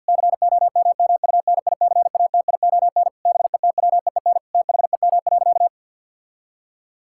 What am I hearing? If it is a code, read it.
Morse code, heard as a keyed tone, about 50 wpm, 700 Hz. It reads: COMMUNICATION BETWEEN THEM.
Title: Sentences from Top 1000 Words - Review Part 6 - 50wpm